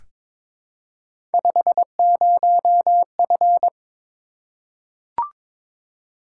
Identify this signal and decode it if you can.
Morse code, heard as a keyed tone, about 22 wpm, 700 Hz. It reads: 50F
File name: practice / 22wpm / Three Letter-Number Combinations - 22wpm